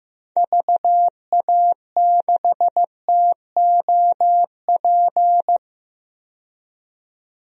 Morse code, heard as a keyed tone, about 15 wpm, 700 Hz. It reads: VA6TOP
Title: DXCC-100 Call Signs - 6 Characters - 15wpm